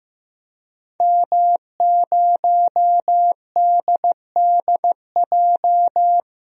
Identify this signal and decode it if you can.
Morse code, heard as a keyed tone, about 15 wpm, 700 Hz. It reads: M0DDJ
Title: DXCC-100 Call Signs - 5 Characters - 15wpm